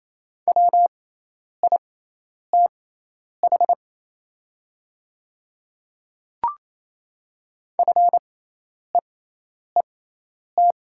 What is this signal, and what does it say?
Morse code, heard as a keyed tone, about 27 wpm, spaced as for 10 wpm, 700 Hz. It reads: WITH FEET
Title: 6x Character Spacing - Top 300 Words - 28wpm